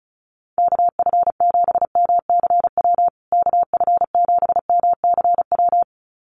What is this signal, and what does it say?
Morse code, heard as a keyed tone, about 34 wpm, 700 Hz. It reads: KF7MCW KF7MCW
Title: USA Call Signs Repeated 2x 35wpm